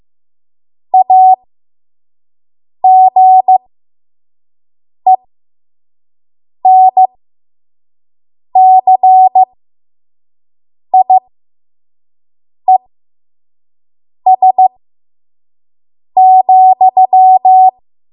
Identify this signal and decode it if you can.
Morse code, heard as a keyed tone, about 15 wpm, spaced as for 5 wpm, 750 Hz. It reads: AGENCIES,
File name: Morse / 05WPM